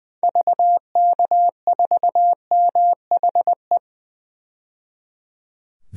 Morse code, heard as a keyed tone, about 20 wpm, 700 Hz. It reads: VK4MHE
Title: DXCC-100 Call Signs - 6 Characters - 20wpm